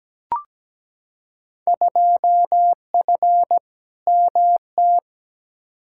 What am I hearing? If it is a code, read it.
Morse code, heard as a keyed tone, about 17 wpm, 700 Hz. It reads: 2F MT
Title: Field Day Exchanges Encore - 17wpm